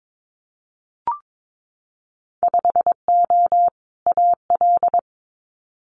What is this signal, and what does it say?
Morse code, heard as a keyed tone, about 22 wpm, 700 Hz. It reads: 5O AL
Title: Winter Field Day Exchanges 22wpm